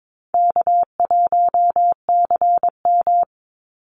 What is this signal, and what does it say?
Morse code, heard as a keyed tone, about 22 wpm, 700 Hz. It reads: K1CM
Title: NA SOTA Chaser Callsigns 22wpm